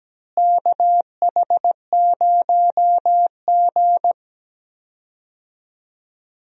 Morse code, heard as a keyed tone, about 17 wpm, 700 Hz. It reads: KH0G
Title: USA Call Signs - 4 Characters - Repeated 3x - 17wpm